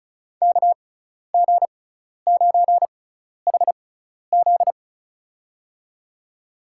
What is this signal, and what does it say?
Morse code, heard as a keyed tone, about 34 wpm, spaced as for 12 wpm, 700 Hz. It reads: KG9HZ
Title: USA Call Signs - 6x Character Spacing - 35wpm